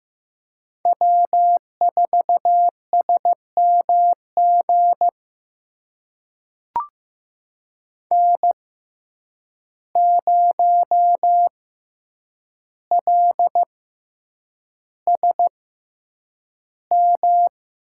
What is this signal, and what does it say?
Morse code, heard as a keyed tone, about 15 wpm, spaced as for 8 wpm, 700 Hz. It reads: W4SMG N0LSM
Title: USA Call Signs - 5 Characters - 6x Character Spacing - 15wpm